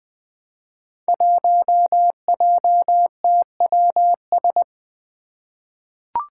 Morse code, heard as a keyed tone, about 20 wpm, 700 Hz. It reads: 1JTWS
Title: Five Challenging Letter-Number Combinations 20wpm